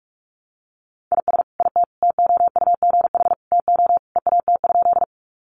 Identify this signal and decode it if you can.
Morse code, heard as a keyed tone, about 44 wpm, 700 Hz. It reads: IS IT TOUGH TO EAT?
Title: Sentences from Top 1200 Words 45wpm